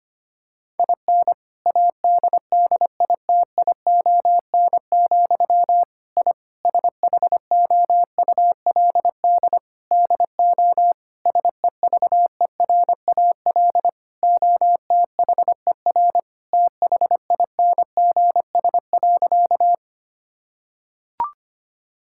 Morse code, heard as a keyed tone, about 25 wpm, 700 Hz. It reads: IN ADDITION, I SHOULD DO SEVERAL OTHER THINGS.
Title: Sentences from Top 1000 Words - Review Part 6 - 25wpm